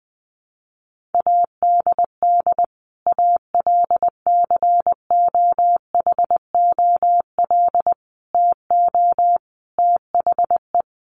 Morse code, heard as a keyed tone, about 19 wpm, 700 Hz. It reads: ADD ALCOHOL TO THE
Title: Sentences from Top 1500 Words 20wpm